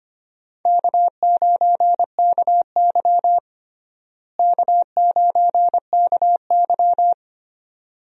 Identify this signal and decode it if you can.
Morse code, heard as a keyed tone, about 25 wpm, 700 Hz. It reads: K9KY K9KY
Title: USA Call Signs - 4 Characters - Repeated 2x - 25wpm